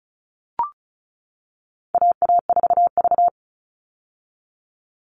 Morse code, heard as a keyed tone, about 34 wpm, 700 Hz. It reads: AA4V
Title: All Cut Numbers - 4-Digit 35wpm